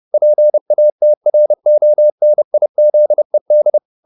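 Morse code, heard as a keyed tone, about 30 wpm, 600 Hz. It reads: PATRONIZED
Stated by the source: Morse code at 30 wpm, the novel Never by Ken Follett in English, from CW med Gnister.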